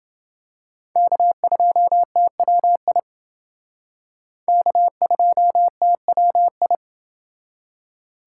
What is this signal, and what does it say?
Morse code, heard as a keyed tone, about 28 wpm, 700 Hz. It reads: K2TWI K2TWI